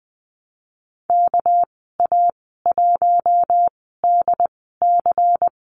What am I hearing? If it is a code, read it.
Morse code, heard as a keyed tone, about 20 wpm, spaced as for 14 wpm, 700 Hz. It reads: KA1DC
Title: 200 Lesson 32g - Letter k - Call Signs 20wpm